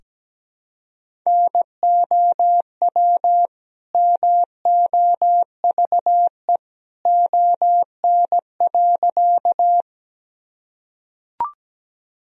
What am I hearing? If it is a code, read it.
Morse code, heard as a keyed tone, about 17 wpm, 700 Hz. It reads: NOW MOVE ON.
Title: Sentences from Top 1000 Words - Review Part 1 - 17wpm